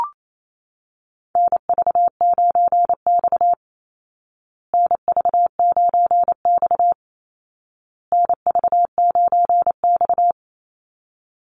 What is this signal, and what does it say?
Morse code, heard as a keyed tone, about 28 wpm, 700 Hz. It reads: NV9X NV9X NV9X
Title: Call Signs Repeated 2x - 28wpm